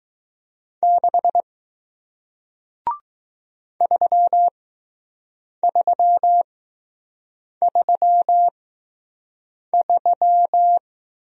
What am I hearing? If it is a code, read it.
Morse code, heard as a keyed tone, about 20 wpm, spaced as for 6 wpm, 700 Hz. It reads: 6 3333